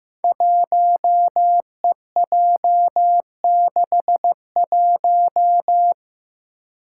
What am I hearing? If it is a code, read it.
Morse code, heard as a keyed tone, about 15 wpm, 700 Hz. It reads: 1EJ61
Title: Five Challenging Letter-Number Combinations 15wpm